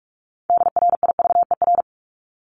Morse code, heard as a keyed tone, about 44 wpm, 700 Hz. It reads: DRIVER
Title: Top 751-1000 Words 45wpm